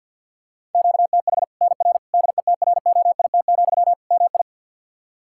Morse code, heard as a keyed tone, about 52 wpm, 700 Hz. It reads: QTH NR DETROIT, MI
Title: Practice QSO - Exchange 1 - Beginning - 50wpm